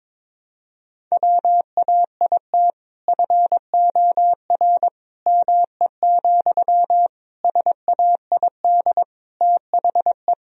Morse code, heard as a keyed tone, about 22 wpm, 700 Hz. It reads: WAIT FOR ME, SAID THE
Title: Sentences from Top 1000 Words - Review Part 4 - 22wpm